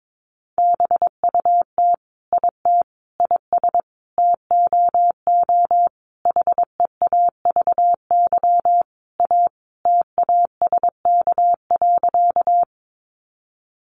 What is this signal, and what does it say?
Morse code, heard as a keyed tone, about 22 wpm, 700 Hz. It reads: BUT IT IS TOO HEAVY A TASK.
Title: Sentences from Top 1100 Words 22wpm